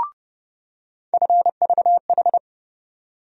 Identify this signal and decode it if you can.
Morse code, heard as a keyed tone, about 32 wpm, 700 Hz. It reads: FVH
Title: USA Call Sign Postfixes - 3 Letters 30wpm